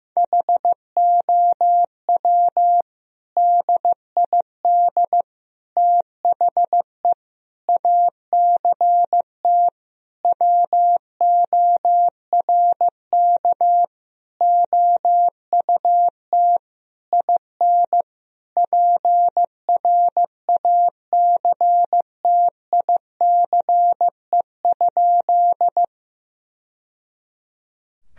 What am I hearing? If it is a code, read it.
Morse code, heard as a keyed tone, about 16 wpm, 700 Hz. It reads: HOW DID THE ACT WORK OUT IN PRACTICE?
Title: Sentences from Top 1000 Words - Review Part 4 - 15wpm